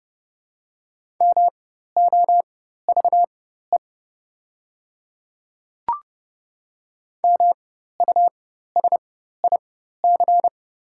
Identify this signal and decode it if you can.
Morse code, heard as a keyed tone, about 32 wpm, spaced as for 14 wpm, 700 Hz. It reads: MOVE MUSIC
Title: 4x Character Spacing - Top 300 Words - 30wpm